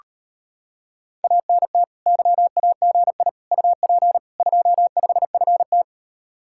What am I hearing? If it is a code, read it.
Morse code, heard as a keyed tone, about 32 wpm, 700 Hz. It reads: ANT YAGI UP 25FT